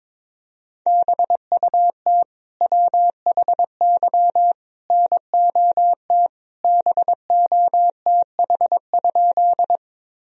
Morse code, heard as a keyed tone, about 21 wpm, 700 Hz. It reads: BUT WHY NOT BOTH?
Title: Sentences from Top 1000 Words - Review Part 1 - 22wpm